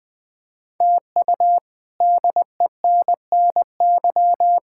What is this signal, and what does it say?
Morse code, heard as a keyed tone, about 20 wpm, 700 Hz. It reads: TU DENNY
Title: K1USN SST QSO Practice - 20wpm